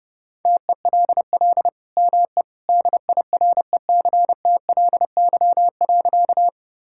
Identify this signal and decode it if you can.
Morse code, heard as a keyed tone, about 30 wpm, 700 Hz. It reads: TELL ME DIRECTLY.
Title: Sentences from Top 1000 Words - Review Part 5 - 30wpm